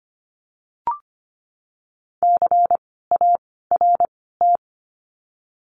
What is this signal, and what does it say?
Morse code, heard as a keyed tone, about 25 wpm, spaced as for 16 wpm, 700 Hz. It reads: CART
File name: course / 25:10wpm / 094 Lesson 16f - Letter c - Words 25wpm